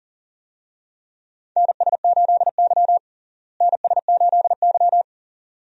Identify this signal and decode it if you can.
Morse code, heard as a keyed tone, about 37 wpm, 700 Hz. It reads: NS8Y NS8Y
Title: USA Call Signs - 4 Characters - Repeated 2x - 40wpm